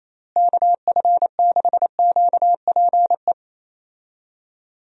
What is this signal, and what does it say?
Morse code, heard as a keyed tone, about 28 wpm, 700 Hz. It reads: KF6QPE
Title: USA Call Signs - 6 Characters - Repeated 3x - 28wpm